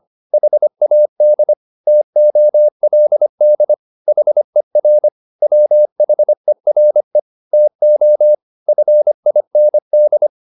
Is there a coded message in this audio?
Yes: Morse code from 0.3 to 10.3 s, heard as a keyed tone, about 25 wpm, 600 Hz. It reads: HAD TOLD HER WHERE TO FIND